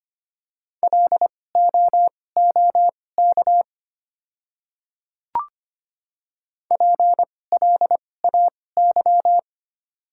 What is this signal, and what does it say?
Morse code, heard as a keyed tone, about 25 wpm, spaced as for 18 wpm, 690 Hz. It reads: LOOK PLAY
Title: 2x Character Spacing - Top 300 Words - 25wpm